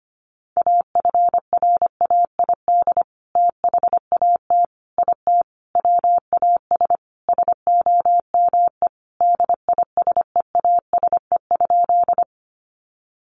Morse code, heard as a keyed tone, about 25 wpm, 700 Hz. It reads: AFRAID THAT IT WAS SOME DISEASE?
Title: Sentences from Top 1500 Words 25wpm